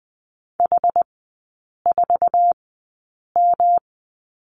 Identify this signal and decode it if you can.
Morse code, heard as a keyed tone, about 20 wpm, spaced as for 8 wpm, 700 Hz. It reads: H4M
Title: 114 Lesson 19e - Letter w - Three Characters 20wpm